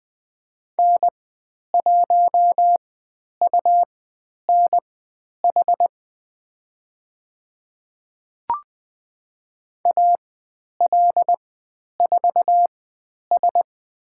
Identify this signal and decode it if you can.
Morse code, heard as a keyed tone, about 20 wpm, spaced as for 10 wpm, 700 Hz. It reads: N1UNH AL4S
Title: 102 Lesson 17g - Letter u - Call Signs 20wpm